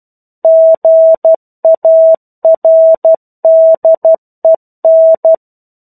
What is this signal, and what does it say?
Morse code, heard as a keyed tone, about 12 wpm, 650 Hz. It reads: GARDEN
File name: 01-Chapter-01 12 WPM